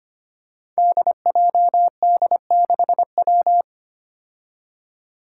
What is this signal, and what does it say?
Morse code, heard as a keyed tone, about 25 wpm, 700 Hz. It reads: DJD6W
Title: Five Challenging Letter-Number Combinations 25wpm